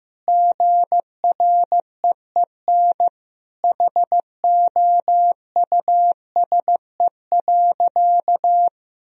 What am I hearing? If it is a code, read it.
Morse code, heard as a keyed tone, about 15 wpm, 700 Hz. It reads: GREEN HOUSE.
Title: Sentences from Top 1000 Words - Review Part 2 - 15wpm